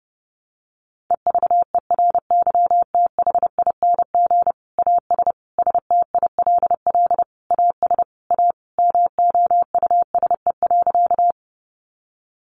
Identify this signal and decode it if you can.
Morse code, heard as a keyed tone, about 30 wpm, 700 Hz. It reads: EVERYTHING AS STILL AS A MOUSE.
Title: Sentences from Top 700 Words 30wpm